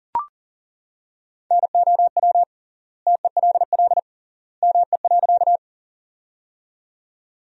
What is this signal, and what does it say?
Morse code, heard as a keyed tone, about 40 wpm, 700 Hz. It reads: NOW TELL ME.
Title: 3x Word-Spacing - Sentences from Top 200 Words - 40wpm